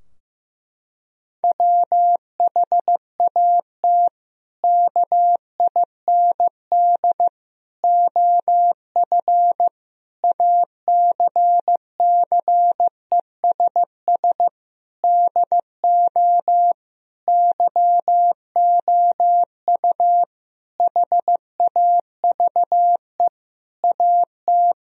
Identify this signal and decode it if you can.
Morse code, heard as a keyed tone, about 16 wpm, 700 Hz. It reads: WHAT KIND OF ACCESS DO YOU HAVE AT
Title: Sentences from Top 1000 Words - Review Part 4 - 15wpm